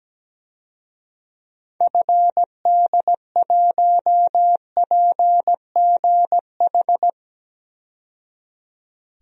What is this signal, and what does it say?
Morse code, heard as a keyed tone, about 17 wpm, 700 Hz. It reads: FD1PGH
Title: DXCC-100 Call Signs - 6 Characters - 17wpm